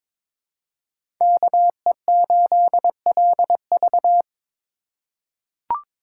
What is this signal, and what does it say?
Morse code, heard as a keyed tone, about 21 wpm, 700 Hz. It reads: KE8LV
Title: USA Call Signs - 3x Character Spacing - 22wpm